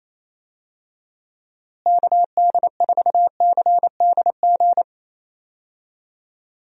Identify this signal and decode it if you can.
Morse code, heard as a keyed tone, about 28 wpm, 700 Hz. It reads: KD4CDG
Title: USA Call Signs - 6 Characters - Repeated 2x - 28wpm